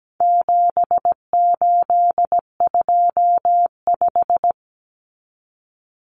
Morse code, heard as a keyed tone, about 17 wpm, 700 Hz. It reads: 7825
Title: Numbers - 4-Digit 17wpm